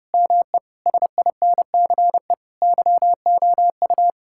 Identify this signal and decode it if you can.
Morse code, heard as a keyed tone, about 30 wpm, 700 Hz. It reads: ME SINCE YOU
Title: Sentences from Top 1600 Words 30wpm